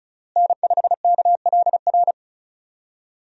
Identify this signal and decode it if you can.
Morse code, heard as a keyed tone, about 34 wpm, 700 Hz. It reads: N5KLR
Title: USA Call Signs - 5 Characters - Repeated 3x - 35wpm